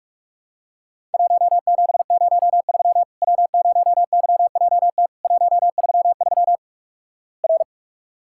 Morse code, heard as a keyed tone, about 46 wpm, 700 Hz. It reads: 1703 W0YJT 133 R
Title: Worked All Europe DX Contest - QTC Exchange - 45wpm